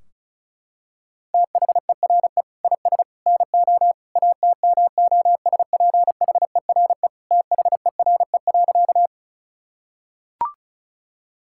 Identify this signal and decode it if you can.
Morse code, heard as a keyed tone, about 36 wpm, 700 Hz. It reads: THERE IS NO ATMOSPHERE THERE.
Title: Sentences from Top 1600 Words 35wpm